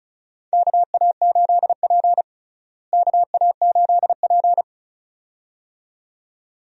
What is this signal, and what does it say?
Morse code, heard as a keyed tone, about 36 wpm, 700 Hz. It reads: KA8P KA8P
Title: USA Call Signs - 4 Characters - Repeated 2x - 35wpm